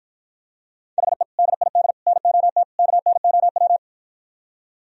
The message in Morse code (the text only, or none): HE DID NOT KNOW